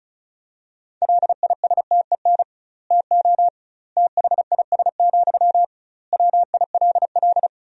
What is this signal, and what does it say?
Morse code, heard as a keyed tone, about 36 wpm, 700 Hz. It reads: LISTEN TO THIS, WILL
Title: 2x Word-Spacing - Sentences from Top 400 Words - 35wpm